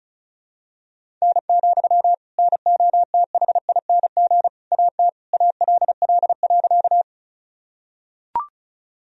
Morse code, heard as a keyed tone, about 34 wpm, 700 Hz. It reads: N, NOTHING AT ALL.
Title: Sentences from Top 500 Words and Abbreviations 35wpm